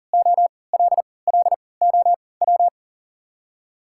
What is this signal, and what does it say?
Morse code, heard as a keyed tone, about 40 wpm, spaced as for 23 wpm, 700 Hz. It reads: OLLOW
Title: 3x Character Spacing - Top 300 Words - 40wpm